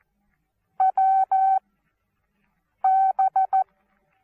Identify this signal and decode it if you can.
Morse code, heard as a keyed tone, about 14 wpm, spaced as for 6 wpm, 770 Hz. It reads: WB
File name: Onlinelehrgang / morsen / DJ4UF Morsen19